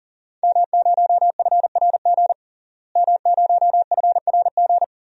Morse code, heard as a keyed tone, about 40 wpm, 700 Hz. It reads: M0FRG M0FRG
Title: DXCC-100 Call Signs - 5 Characters - Repeated 2x - 40wpm